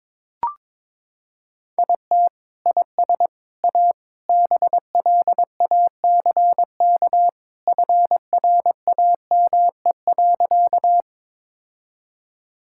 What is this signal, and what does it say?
Morse code, heard as a keyed tone, about 22 wpm, 700 Hz. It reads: IT IS A BLACK FRAME.
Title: Sentences from Top 1000 Words - Review Part 6 - 22wpm